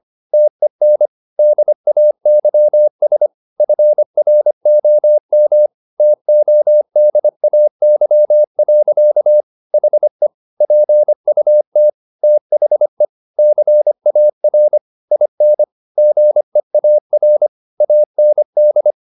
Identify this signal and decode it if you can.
Morse code, heard as a keyed tone, about 25 wpm, 600 Hz. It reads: TEN DAYS FROM TODAY. HE PUT THE CAR IN GEAR AND